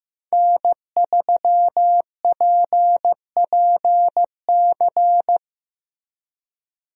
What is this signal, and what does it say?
Morse code, heard as a keyed tone, about 15 wpm, 700 Hz. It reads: N3PPC
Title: USA Call Signs - 5 Characters - 15wpm